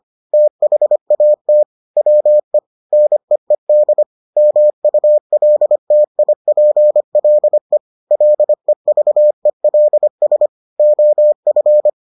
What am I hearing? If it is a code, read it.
Morse code, heard as a keyed tone, about 25 wpm, 600 Hz. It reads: THAT WE NEED MULTIPLE LEVELS OF